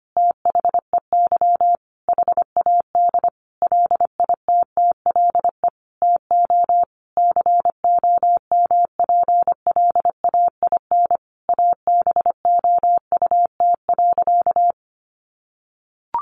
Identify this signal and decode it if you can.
Morse code, heard as a keyed tone, about 25 wpm, 700 Hz. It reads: THEY HAD LITTLE TO COMPLAIN ABOUT.
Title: Sentences from Top 1300 Words 25wpm